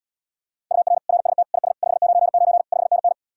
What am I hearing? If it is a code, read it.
Morse code, heard as a keyed tone, about 80 wpm, 700 Hz. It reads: UR RST IS 599 5NN